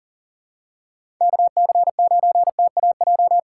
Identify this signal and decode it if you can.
Morse code, heard as a keyed tone, about 39 wpm, 700 Hz. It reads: KC9TAJ
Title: USA Call Signs - 6 Characters - 6x Character Spacing - 40wpm